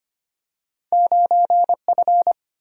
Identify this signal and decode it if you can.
Morse code, heard as a keyed tone, about 25 wpm, 690 Hz. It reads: 9F
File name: Two Letter-Number Combinations - 25wpm